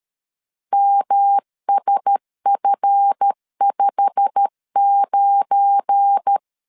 Morse code, heard as a keyed tone, about 13 wpm, 800 Hz. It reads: MSF59
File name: LEZIONI / 9 2867 12wpm 2322